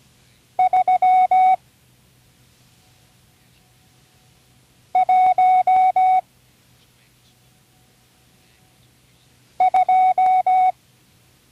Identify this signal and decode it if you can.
Morse code, heard as a keyed tone, about 17 wpm, 740 Hz. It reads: 3 1 2
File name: CW Learning Programs / Learning Morse Code (from Tune in the World) / 34 random numbers 1-5